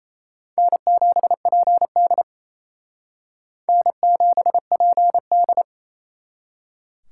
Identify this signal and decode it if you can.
Morse code, heard as a keyed tone, about 35 wpm, 700 Hz. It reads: N7PD N7PD
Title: Speed Racing - USA 1x2 Call Signs 28wpm